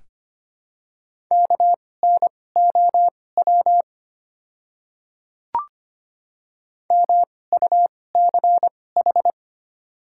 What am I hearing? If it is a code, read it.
Morse code, heard as a keyed tone, about 25 wpm, spaced as for 18 wpm, 700 Hz. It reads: KNOW MUCH
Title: 2x Character Spacing - Top 200 Words - 25wpm